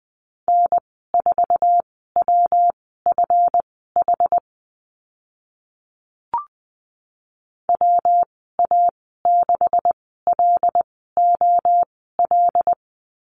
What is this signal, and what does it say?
Morse code, heard as a keyed tone, about 20 wpm, spaced as for 14 wpm, 700 Hz. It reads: N4WFH WA6LOL
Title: 140 Lesson 23g - Letter f - Call Signs 20wpm